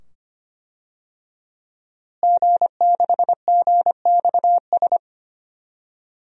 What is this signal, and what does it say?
Morse code, heard as a keyed tone, about 25 wpm, 700 Hz. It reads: G6GXS
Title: DXCC-100 Call Signs - 5 Characters - Repeated 2x - 25wpm